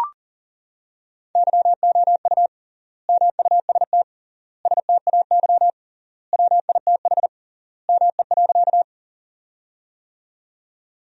Morse code, heard as a keyed tone, about 40 wpm, 700 Hz. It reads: YOU MUST STAY WITH ME.
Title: 3x Word-Spacing - Sentences from Top 500 Words - 40wpm